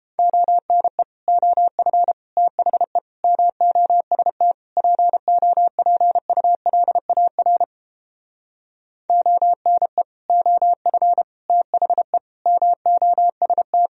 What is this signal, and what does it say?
Morse code, heard as a keyed tone, about 31 wpm, 700 Hz. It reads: ONE OF THE MOST POPULAR ONE OF THE MOST